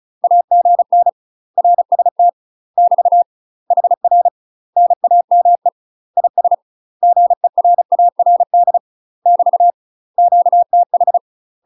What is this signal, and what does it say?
Morse code, heard as a keyed tone, about 34 wpm, 700 Hz. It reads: AGN RST <BT> HR NAME IS GERARD <BT> QTH
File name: UK MASTER QSOs TRAINING/35WPM